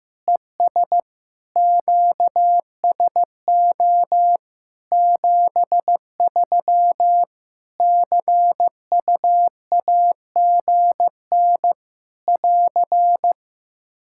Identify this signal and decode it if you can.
Morse code, heard as a keyed tone, about 15 wpm, 700 Hz. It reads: ES QSO 73 CUAGN <AR>
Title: Practice QSO - Exchange 4 - Ending - 15wpm